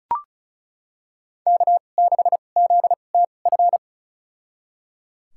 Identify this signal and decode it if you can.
Morse code, heard as a keyed tone, about 36 wpm, spaced as for 25 wpm, 700 Hz. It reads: K6ZTF
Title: USA Call Signs - 5 Characters - 2x Character Spacing - 35wpm